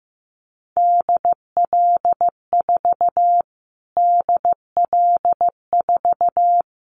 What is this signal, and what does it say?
Morse code, heard as a keyed tone, about 15 wpm, 700 Hz. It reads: DL4 DL4
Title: Callsign Prefixes for Top 25 DXCC - Repeated 3x - 15wpm